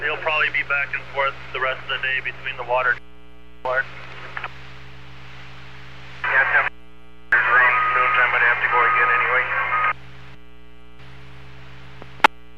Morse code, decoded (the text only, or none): none